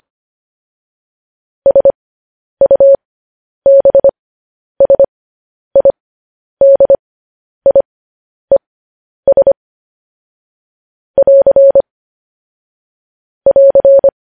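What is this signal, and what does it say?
Morse code, heard as a keyed tone, about 25 wpm, spaced as for 10 wpm, 550 Hz. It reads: SUBSIDIES <AR> <AR>
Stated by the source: Morse code transcription of news headlines sent at 10 wpm.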